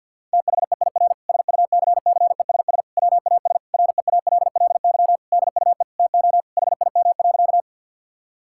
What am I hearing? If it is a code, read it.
Morse code, heard as a keyed tone, about 46 wpm, 700 Hz. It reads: THEIR SUCCESS WAS REALLY DUE TO HIM.
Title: Sentences from Top 1000 Words - Review Part 6 - 50wpm